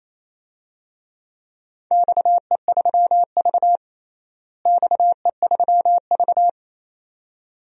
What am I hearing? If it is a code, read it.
Morse code, heard as a keyed tone, about 28 wpm, 700 Hz. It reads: XE3V XE3V
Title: DXCC-100 Call Signs - 4 Characters - Repeated 2x - 28wpm